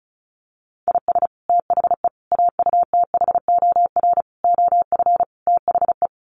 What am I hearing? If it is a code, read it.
Morse code, heard as a keyed tone, about 34 wpm, 700 Hz. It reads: IS THE AUTHOR OF THE
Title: Sets of 5 Words Encore 35wpm